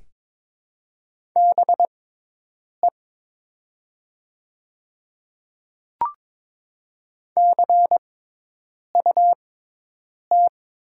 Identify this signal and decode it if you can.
Morse code, heard as a keyed tone, about 22 wpm, spaced as for 8 wpm, 700 Hz. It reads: BE CUT
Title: 6x Character Spacing - Top 500 Words - 22wpm